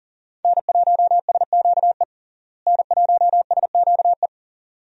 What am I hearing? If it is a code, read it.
Morse code, heard as a keyed tone, about 40 wpm, 700 Hz. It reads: N1SQE N1SQE
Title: USA Call Signs - 5 Characters - Repeated 2x - 40wpm